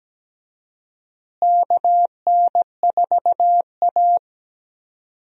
Morse code, heard as a keyed tone, about 17 wpm, 700 Hz. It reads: KN4A
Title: USA Call Signs - 4 Characters - Repeated 3x - 17wpm